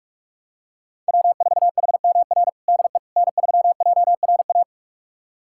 Morse code, heard as a keyed tone, about 41 wpm, 700 Hz. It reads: W4HMR DE N3JRA
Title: USA Callsign DE Callsign - 5 Characters - 45wpm